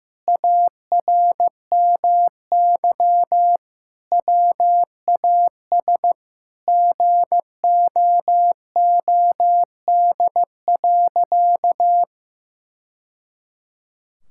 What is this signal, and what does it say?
Morse code, heard as a keyed tone, about 16 wpm, 700 Hz. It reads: ARMY WAS GOOD.
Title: Sentences from Top 1000 Words - Review Part 5 - 15wpm